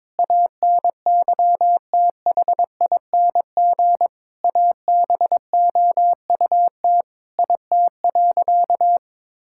Morse code, heard as a keyed tone, about 21 wpm, 700 Hz. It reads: ANYTHING ABOUT IT.
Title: Sentences from Top 1600 Words 22wpm